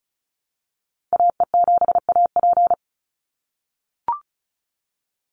Morse code, heard as a keyed tone, about 36 wpm, 700 Hz. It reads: AE7AP
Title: NA SOTA Activator Callsigns 35wpm